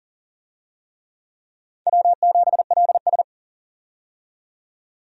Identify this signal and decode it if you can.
Morse code, heard as a keyed tone, about 40 wpm, 700 Hz. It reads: W7LS
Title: USA Call Signs - 4 Characters - 40wpm